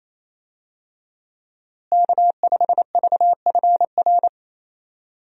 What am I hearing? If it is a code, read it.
Morse code, heard as a keyed tone, about 28 wpm, 700 Hz. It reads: K5VFR
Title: USA Call Signs - 5 Characters - Repeated 2x - 28wpm